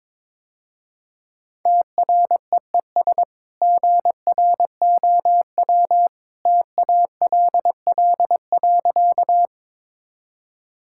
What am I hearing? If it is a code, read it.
Morse code, heard as a keyed tone, about 22 wpm, 700 Hz. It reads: TREES GROW TALL.
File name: practice / 22wpm / Easy Sentences with Hints - 22wpm